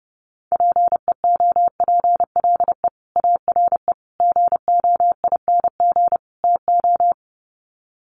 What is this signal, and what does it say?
Morse code, heard as a keyed tone, about 30 wpm, 700 Hz. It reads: PEOPLE ARE GOING TO
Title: Sets of 4 Words 30wpm